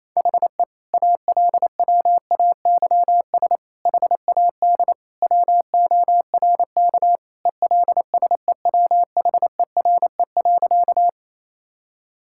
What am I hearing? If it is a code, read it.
Morse code, heard as a keyed tone, about 27 wpm, 700 Hz. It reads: HE ALWAYS HAD WORK ELSEWHERE.
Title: Sentences from Top 1600 Words 28wpm